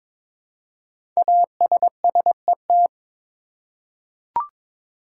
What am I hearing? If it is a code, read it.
Morse code, heard as a keyed tone, about 22 wpm, 700 Hz. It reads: ASSET
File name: 5 Letter Words - First 3 Letters Given - 22wpm